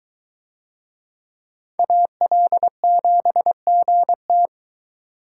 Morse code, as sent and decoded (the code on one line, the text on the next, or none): .- .-.. --... --. -
AL7GT